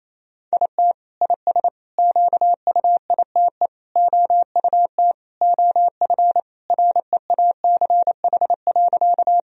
Morse code, heard as a keyed tone, about 27 wpm, 700 Hz. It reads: IT IS QUITE OUT OF REACH.
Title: Sentences from Top 1000 Words - Review Part 2 - 28wpm